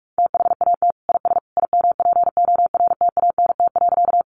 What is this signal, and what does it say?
Morse code, heard as a keyed tone, about 44 wpm, 700 Hz. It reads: THAT IS IMPORTANT.
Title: Sentences from Top 600 Words 45wpm